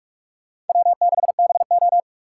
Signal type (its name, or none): Morse code